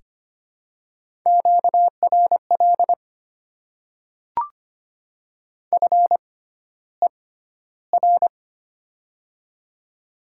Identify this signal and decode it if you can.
Morse code, heard as a keyed tone, about 25 wpm, spaced as for 13 wpm, 700 Hz. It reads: QRL FER
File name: Top 20 QSO Elements - 6x Character Spacing - 25wpm